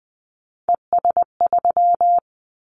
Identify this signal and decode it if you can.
Morse code, heard as a keyed tone, about 20 wpm, 700 Hz. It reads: ES3
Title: Callsign Prefixes for Top 50 DXCC - 20wpm